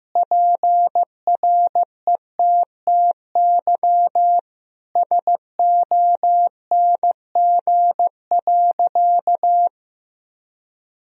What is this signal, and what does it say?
Morse code, heard as a keyed tone, about 15 wpm, 700 Hz. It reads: PRETTY SONG.
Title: Sentences from Top 1000 Words - Review Part 2 - 15wpm